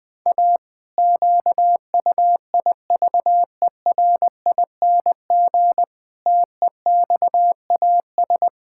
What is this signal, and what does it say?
Morse code, heard as a keyed tone, about 21 wpm, 700 Hz. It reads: A QUIVERING TEXAS
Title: Pangram Challenge v2 - 20wpm